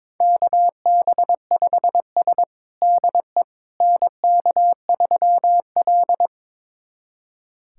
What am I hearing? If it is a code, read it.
Morse code, heard as a keyed tone, about 22 wpm, 700 Hz. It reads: KB5S DE NK3L
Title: USA Callsign DE Callsign - 4 Characters - 22wpm